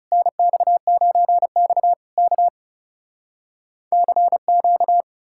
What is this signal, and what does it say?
Morse code, heard as a keyed tone, about 38 wpm, 700 Hz. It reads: NX9X K CQ
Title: Speed-Racing Common QSO Phrases Encore - 30wpm